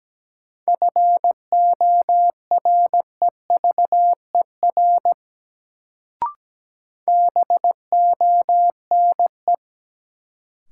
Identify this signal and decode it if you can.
Morse code, heard as a keyed tone, about 17 wpm, 700 Hz. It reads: FOREVER BONE
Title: Sentences from Top 1600 Words - New Word Drill - 17wpm